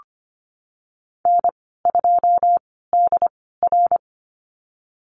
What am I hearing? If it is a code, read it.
Morse code, heard as a keyed tone, about 25 wpm, spaced as for 16 wpm, 700 Hz. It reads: N2DR
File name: Lesson 15g - Number 5 - Call Signs 25wpm